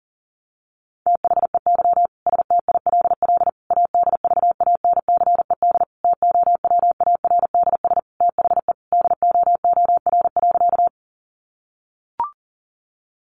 Morse code, heard as a keyed tone, about 37 wpm, 700 Hz. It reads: THEY STILL ADVANCED TOWARDS THE DOOR.
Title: Sentences from Top 1200 Words 40wpm